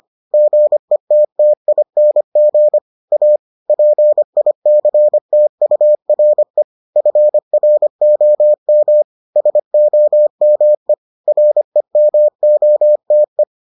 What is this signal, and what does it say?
Morse code, heard as a keyed tone, about 25 wpm, 600 Hz. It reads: GETTING A PICTURE FROM SOME REMOTE